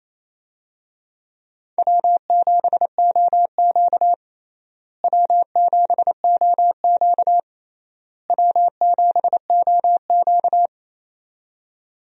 Morse code, heard as a keyed tone, about 27 wpm, 700 Hz. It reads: W7OQ W7OQ W7OQ
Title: USA Call Signs - 4 Characters - Repeated 3x - 28wpm